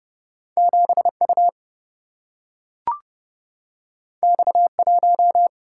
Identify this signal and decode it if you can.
Morse code, heard as a keyed tone, about 32 wpm, 700 Hz. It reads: 7 U X 1
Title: Single-Letter Single-Number 30wpm